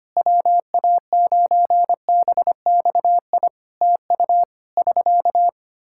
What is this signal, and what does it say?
Morse code, heard as a keyed tone, about 25 wpm, 700 Hz. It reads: WA9BXI TU <SK>
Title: K1USN SST QSO Practice 25wpm